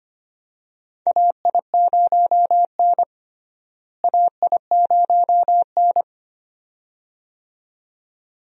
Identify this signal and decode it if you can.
Morse code, heard as a keyed tone, about 25 wpm, 700 Hz. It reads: AI0N AI0N
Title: USA Call Signs - 4 Characters - Repeated 2x - 25wpm